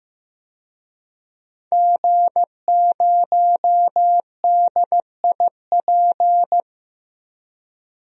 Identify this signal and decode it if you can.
Morse code, heard as a keyed tone, about 15 wpm, 700 Hz. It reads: G0DIP
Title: DXCC-100 Call Signs - 5 Characters - Repeated 2x - 15wpm